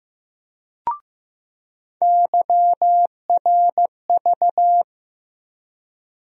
Morse code, heard as a keyed tone, about 15 wpm, 700 Hz. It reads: YRV
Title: USA Call Sign Postfixes - 3 Letters 15wpm